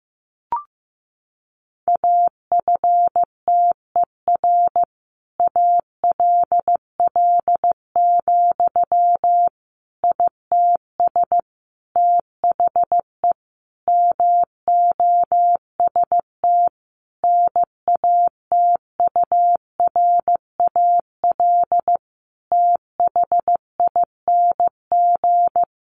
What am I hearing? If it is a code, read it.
Morse code, heard as a keyed tone, about 15 wpm, 700 Hz. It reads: AFTER ALL, ITS THE MOST NATURAL THING
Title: Sentences from Top 700 Words 15wpm